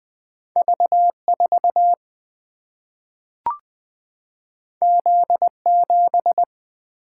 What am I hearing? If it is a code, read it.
Morse code, heard as a keyed tone, about 20 wpm, 700 Hz. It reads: V4 Z7
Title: Single-Letter Single-Number 20wpm